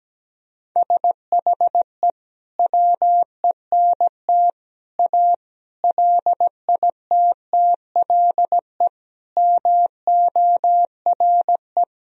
Morse code, heard as a keyed tone, about 17 wpm, 700 Hz. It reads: SHE WENT A LITTLE MORE
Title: Sentences from Top 1000 Words - Review Part 2 - 17wpm